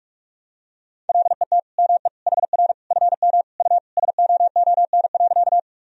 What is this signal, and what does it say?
Morse code, heard as a keyed tone, about 45 wpm, 700 Hz. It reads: LET ME HR FM U SOON.
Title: Sentences from Top 400 Words and Abbreviations 45wpm